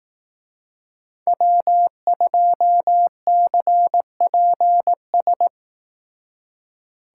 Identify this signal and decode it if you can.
Morse code, heard as a keyed tone, about 18 wpm, 700 Hz. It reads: W2CPS
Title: Speed Racing - USA 1x3 Call Signs 15wpm